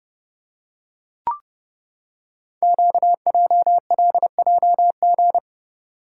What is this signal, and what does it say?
Morse code, heard as a keyed tone, about 30 wpm, 690 Hz. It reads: QJLJG
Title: Code Groups - 5 Letters - 30wpm